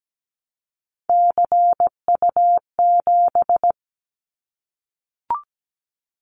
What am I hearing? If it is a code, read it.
Morse code, heard as a keyed tone, about 17 wpm, 700 Hz. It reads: CU7
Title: Callsign Prefixes for Top 100 DXCC - 17wpm